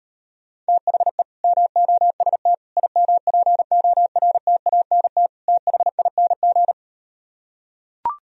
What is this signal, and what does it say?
Morse code, heard as a keyed tone, about 38 wpm, 700 Hz. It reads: THE MOST IMPORTANT THING